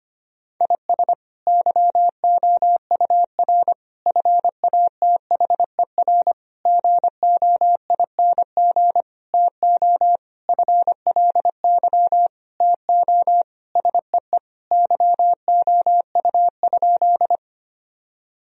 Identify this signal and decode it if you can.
Morse code, heard as a keyed tone, about 25 wpm, 700 Hz. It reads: IS YOUR FATHER GOING TO FLY TO SEE YOU?
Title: Sentences from Top 1000 Words - Review Part 2 - 25wpm